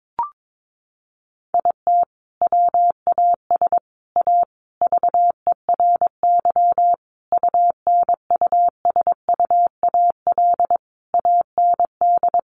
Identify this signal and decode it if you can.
Morse code, heard as a keyed tone, about 22 wpm, 700 Hz. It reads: IT WAS A VERY UNUSUAL AND
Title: Sentences from Top 1400 Words 22wpm